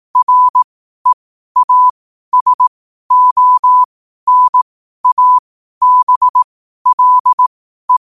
Morse code, heard as a keyed tone, about 18 wpm, spaced as for 12 wpm, 1000 Hz. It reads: REASONABLE
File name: qotdfiles / 12wpm